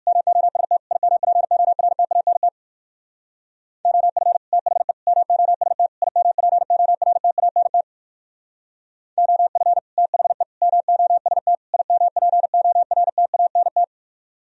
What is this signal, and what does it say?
Morse code, heard as a keyed tone, about 55 wpm, 700 Hz. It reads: MOST IMPORTANT OF THE MOST IMPORTANT OF THE MOST IMPORTANT
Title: Speed-Racing - Sets of 4 Words 45wpm